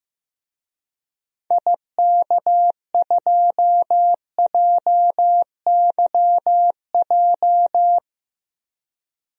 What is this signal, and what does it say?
Morse code, heard as a keyed tone, about 15 wpm, 700 Hz. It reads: IK2JYJ
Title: DXCC-100 Call Signs - 6 Characters - 15wpm